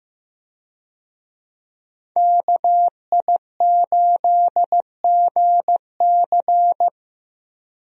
Morse code, heard as a keyed tone, about 16 wpm, 700 Hz. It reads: KI8GC
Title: USA Call Signs - 5 Characters - Repeated 3x - 15wpm